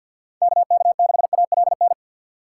Morse code, heard as a keyed tone, about 48 wpm, 700 Hz. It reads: KK6ALN
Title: USA Call Signs - 6 Characters - 3x Character Spacing - 50wpm